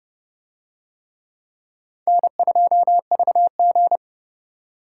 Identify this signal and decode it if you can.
Morse code, heard as a keyed tone, about 29 wpm, 690 Hz. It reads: N2VG